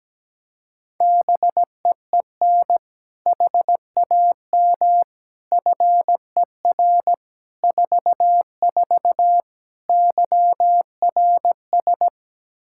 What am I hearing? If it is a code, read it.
Morse code, heard as a keyed tone, about 17 wpm, 700 Hz. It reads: BEEN HAM FER 44 YRS
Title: Practice QSO - Exchange 3 - More Info - 17wpm